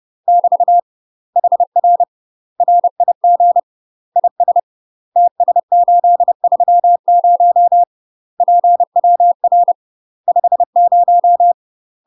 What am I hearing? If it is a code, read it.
Morse code, heard as a keyed tone, about 30 wpm, 700 Hz. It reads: <BT> HR RIG IS TS830 PWR 50
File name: UK MASTER QSOs TRAINING / 30WPM / UK MASTER QSOs 0040